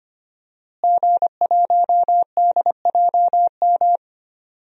Morse code, heard as a keyed tone, about 25 wpm, 700 Hz. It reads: G1DJM
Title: DXCC-100 Call Signs - 5 Characters - Repeated 2x - 25wpm